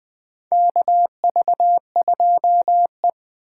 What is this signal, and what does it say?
Morse code, heard as a keyed tone, about 20 wpm, 700 Hz. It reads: KV2E
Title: USA Call Signs - 4 Characters - 8x Character Spacing - 20wpm